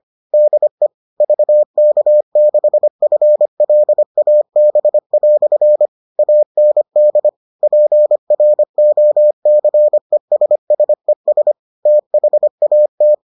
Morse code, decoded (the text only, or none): DE VK6FLAB" AND PROCESSES THAT